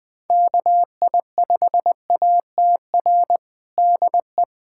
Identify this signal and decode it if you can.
Morse code, heard as a keyed tone, about 21 wpm, 700 Hz. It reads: KI5ATR DE
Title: USA Callsign DE Callsign 20wpm